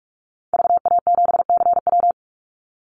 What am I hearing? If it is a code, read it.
Morse code, heard as a keyed tone, about 44 wpm, 700 Hz. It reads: VA7CW
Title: Call Signs Repeated 2x - 45wpm